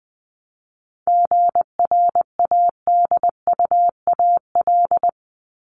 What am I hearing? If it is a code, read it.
Morse code, heard as a keyed tone, about 20 wpm, 700 Hz. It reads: GRADUAL
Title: LY Suffix - Top 100 Words with Base Words 20wpm